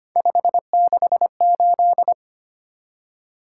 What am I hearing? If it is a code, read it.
Morse code, heard as a keyed tone, about 25 wpm, 700 Hz. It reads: 568
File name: Three Letter-Number Combinations - 25wpm